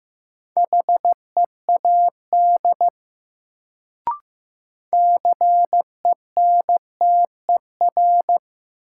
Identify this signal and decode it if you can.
Morse code, heard as a keyed tone, about 15 wpm, 700 Hz. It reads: HEAD CENTER
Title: Sets of 3 Words - New Word Drill - 15wpm